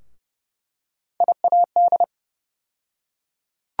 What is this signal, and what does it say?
Morse code, heard as a keyed tone, about 31 wpm, 700 Hz. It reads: IAD
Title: USA Call Sign Postfixes - 3 Letters 30wpm